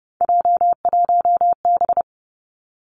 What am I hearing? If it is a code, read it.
Morse code, heard as a keyed tone, about 31 wpm, 700 Hz. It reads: J1B
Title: Three Challenging Letter-Number Combinations 30wpm